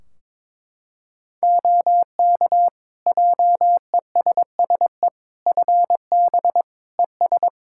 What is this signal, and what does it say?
Morse code, heard as a keyed tone, about 22 wpm, 700 Hz. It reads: OK JESSE FB ES
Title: Practice QSO - Exchange 2 - Info - 22wpm